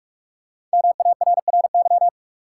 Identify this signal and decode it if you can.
Morse code, heard as a keyed tone, about 46 wpm, 700 Hz. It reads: MARRY
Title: Sentences from Top 1200 Words - New Word Drill - 45wpm